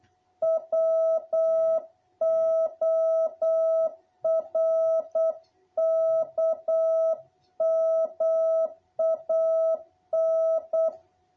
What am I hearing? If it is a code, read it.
Morse code, heard as a keyed tone, about 8 wpm, 660 Hz. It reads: WORKMAN
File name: jh8cbh / 2021cw